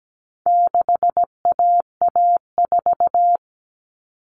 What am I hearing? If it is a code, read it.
Morse code, heard as a keyed tone, about 17 wpm, 700 Hz. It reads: 6AA4
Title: All Cut Numbers - 4-Digit 17wpm